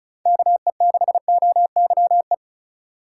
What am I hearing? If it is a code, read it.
Morse code, heard as a keyed tone, about 36 wpm, 700 Hz. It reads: KE6OYE
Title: USA Call Signs - 6 Characters - 4x Character Spacing - 35wpm